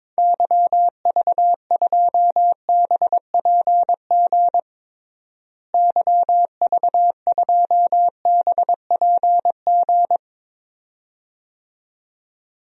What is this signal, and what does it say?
Morse code, heard as a keyed tone, about 22 wpm, 700 Hz. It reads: YV2BPG YV2BPG
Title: DXCC-100 Call Signs - 6 Characters - Repeated 2x - 22wpm